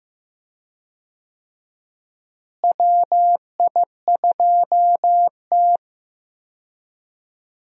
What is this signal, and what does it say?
Morse code, heard as a keyed tone, about 15 wpm, 700 Hz. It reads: WI2T
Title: USA Call Signs - 4 Characters - Repeated 3x - 15wpm